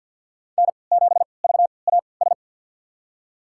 Morse code, heard as a keyed tone, about 52 wpm, spaced as for 29 wpm, 700 Hz. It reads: N7VAS
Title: USA Call Signs - 5 Characters - 3x Character Spacing - 50wpm